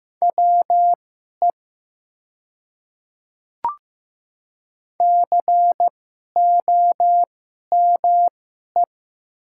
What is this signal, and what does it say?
Morse code, heard as a keyed tone, about 15 wpm, spaced as for 11 wpm, 700 Hz. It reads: WE COME
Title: 2x Character Spacing - Top 200 Words - 15wpm